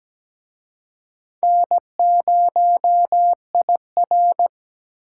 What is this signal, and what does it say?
Morse code, heard as a keyed tone, about 20 wpm, 700 Hz. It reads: N0IR
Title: Speed Racing - USA 1x2 Call Signs 17wpm